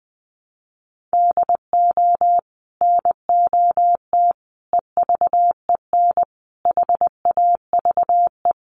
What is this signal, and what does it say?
Morse code, heard as a keyed tone, about 20 wpm, 700 Hz. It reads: DO NOT EVEN HAVE
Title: Sets of 5 Words Encore2 20wpm